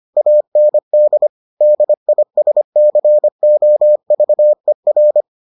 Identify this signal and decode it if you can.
Morse code, heard as a keyed tone, about 25 wpm, 600 Hz. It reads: AND DISCOVER